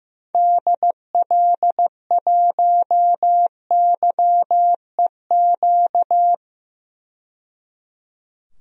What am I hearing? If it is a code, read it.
Morse code, heard as a keyed tone, about 16 wpm, 700 Hz. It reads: DL1YEQ
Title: DXCC-100 Call Signs - 6 Characters - 15wpm